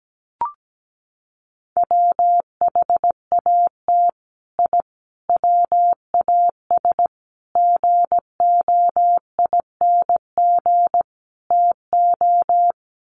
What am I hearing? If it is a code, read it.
Morse code, heard as a keyed tone, about 17 wpm, 700 Hz. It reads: WHAT I WAS GOING TO
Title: Sets of 5 Words Encore2 17wpm